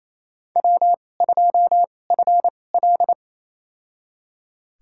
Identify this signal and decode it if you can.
Morse code, heard as a keyed tone, about 27 wpm, spaced as for 20 wpm, 700 Hz. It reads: W2FL
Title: USA Call Signs - 4 Characters - 2x Character Spacing - 28wpm